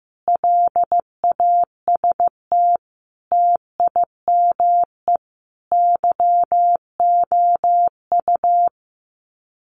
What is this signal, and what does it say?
Morse code, heard as a keyed tone, about 15 wpm, 700 Hz. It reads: LAST TIME YOU
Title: Sets of 5 Words Encore2 15wpm